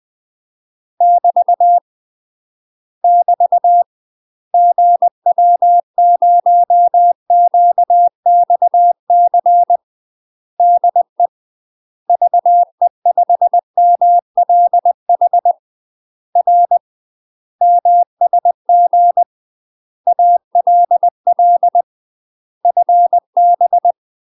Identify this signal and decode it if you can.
Morse code, heard as a keyed tone, about 20 wpm, 700 Hz. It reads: <BT> <BT> GW0QXC DE VE5MLH R MSG ALL FB
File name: UK MASTER QSOs TRAINING / 20WPM / UK MASTER QSOs 0058